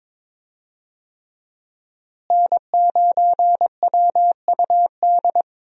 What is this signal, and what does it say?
Morse code, heard as a keyed tone, about 21 wpm, 700 Hz. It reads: N9WUD
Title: USA Call Signs - 5 Characters - Repeated 3x - 22wpm